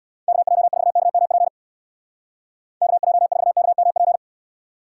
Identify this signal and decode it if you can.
Morse code, heard as a keyed tone, about 66 wpm, 700 Hz. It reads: KJ4KMP KJ4KMP